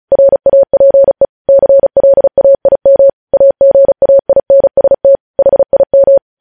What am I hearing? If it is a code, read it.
Morse code, heard as a keyed tone, about 36 wpm, 550 Hz. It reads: RAPE CLAIM AGAINST HIM
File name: podcast / News headlines-1x-35x35